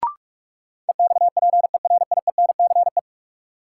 Morse code, heard as a keyed tone, about 41 wpm, 700 Hz. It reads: EXPERIENCE
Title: Sentences from Top 600 Words - New Word Drill - 45wpm